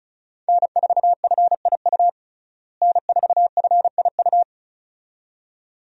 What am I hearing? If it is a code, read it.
Morse code, heard as a keyed tone, about 36 wpm, 700 Hz. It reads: N4FIU N4FIU
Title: USA Call Signs - 5 Characters - Repeated 2x - 35wpm